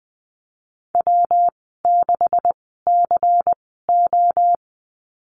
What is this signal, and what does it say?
Morse code, heard as a keyed tone, about 20 wpm, spaced as for 15 wpm, 700 Hz. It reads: W6CO
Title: 128 Lesson 21g - Number 6 - Call Signs 20wpm